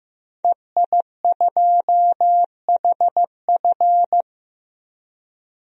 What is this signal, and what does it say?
Morse code, heard as a keyed tone, about 15 wpm, 700 Hz. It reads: EI2HF
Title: DXCC-100 Call Signs - 5 Characters - Repeated 2x - 15wpm